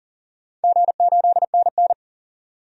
Morse code, heard as a keyed tone, about 40 wpm, 690 Hz. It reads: G8NN
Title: DXCC-100 Call Signs - 4 Characters - 40wpm